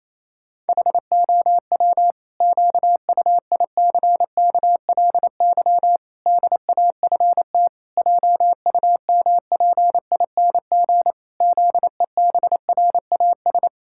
Morse code, heard as a keyed tone, about 28 wpm, 700 Hz. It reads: HOW QUICKLY DAFT JUMPING ZEBRAS
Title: Pangram Challenge v2 - 28wpm